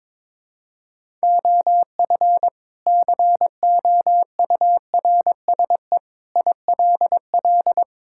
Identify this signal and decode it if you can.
Morse code, heard as a keyed tone, about 22 wpm, 700 Hz. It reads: OF COURSE ILL
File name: practice / 22wpm / Sentences with 2-Word Phrases - 22wpm